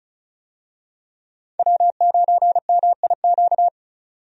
Morse code, heard as a keyed tone, about 36 wpm, 700 Hz. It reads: W9MIQ
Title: USA Call Signs - 5 Characters - 35wpm